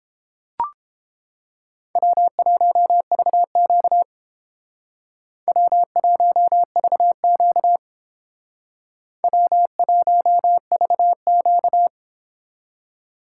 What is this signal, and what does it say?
Morse code, heard as a keyed tone, about 29 wpm, 700 Hz. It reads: W1VQ W1VQ W1VQ